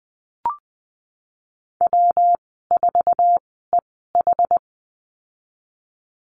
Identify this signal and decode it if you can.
Morse code, heard as a keyed tone, about 20 wpm, spaced as for 15 wpm, 700 Hz. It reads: W4EH
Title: 193 Lesson 31g - Letter v - Call Signs 20wpm